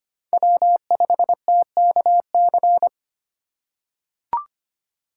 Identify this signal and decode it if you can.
Morse code, heard as a keyed tone, about 25 wpm, 700 Hz. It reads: W5TKC
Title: USA Call Signs - 5 Characters - 3x Character Spacing - 25wpm